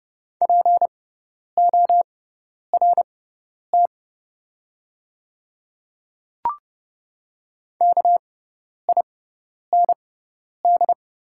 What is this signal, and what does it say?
Morse code, heard as a keyed tone, about 32 wpm, spaced as for 10 wpm, 700 Hz. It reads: PORT KIND
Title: 6x Character Spacing - Top 300 Words - 30wpm